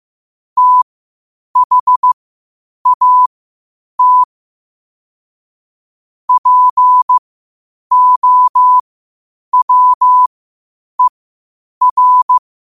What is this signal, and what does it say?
Morse code, heard as a keyed tone, about 15 wpm, spaced as for 8 wpm, 1000 Hz. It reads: THAT POWER
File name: qotdfiles / 8wpm